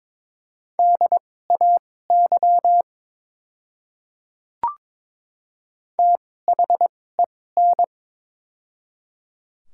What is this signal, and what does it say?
Morse code, heard as a keyed tone, about 22 wpm, spaced as for 16 wpm, 700 Hz. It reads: DAY THEN
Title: 2x Character Spacing - Top 100 Words - 22wpm